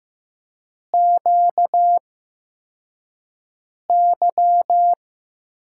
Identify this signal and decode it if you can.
Morse code, heard as a keyed tone, about 15 wpm, spaced as for 4 wpm, 700 Hz. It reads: QY
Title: USA Call Signs - 4 Characters - 8x Character Spacing - 15wpm